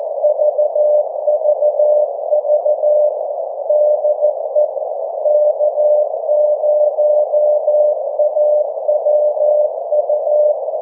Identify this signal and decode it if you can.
Morse code, heard as a keyed tone, about 14 wpm, 600 Hz. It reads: VVV DE K0AWU
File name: 1899khz / k0awu 1742ut-24jan21